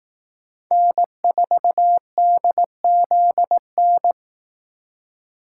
Morse code, heard as a keyed tone, about 18 wpm, 700 Hz. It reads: N4DZN
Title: Speed Racing - USA 1x3 Call Signs 15wpm